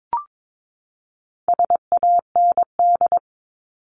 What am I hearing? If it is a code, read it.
Morse code, heard as a keyed tone, about 22 wpm, 700 Hz. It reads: SAND
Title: Top 1501-1750 Words 22wpm